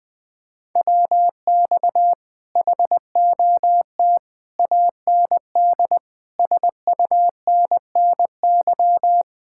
Morse code, heard as a keyed tone, about 20 wpm, 700 Hz. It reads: WX HOT AND SUNNY
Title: Speed-Racing Common QSO Phrases Encore - 17wpm